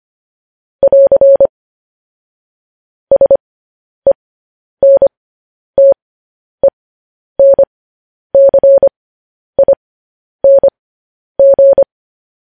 Morse code, heard as a keyed tone, about 25 wpm, spaced as for 10 wpm, 550 Hz. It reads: <AR> SENTENCING
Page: news-headlines-in-morse-code-at-10-wpm